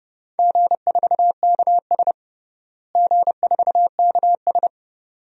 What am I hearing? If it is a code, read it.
Morse code, heard as a keyed tone, about 32 wpm, 700 Hz. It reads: G4KS G4KS
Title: DXCC-100 Call Signs - 4 Characters - Repeated 2x - 30wpm